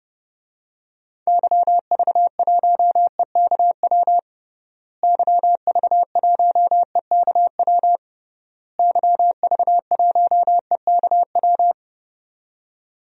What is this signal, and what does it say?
Morse code, heard as a keyed tone, about 30 wpm, 700 Hz. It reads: YV1EKW YV1EKW YV1EKW
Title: DXCC-100 Call Signs - 6 Characters - Repeated 3x - 30wpm